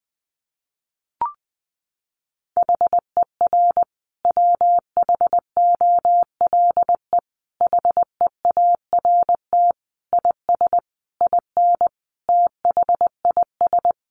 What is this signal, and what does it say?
Morse code, heard as a keyed tone, about 19 wpm, 700 Hz. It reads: HER WHOLE HEART IS IN THIS
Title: Sentences from Top 1500 Words 20wpm